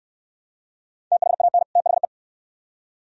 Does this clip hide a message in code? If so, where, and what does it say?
Yes: Morse code from 1.1 to 2.0 s, heard as a keyed tone, about 69 wpm, 700 Hz. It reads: THAN THE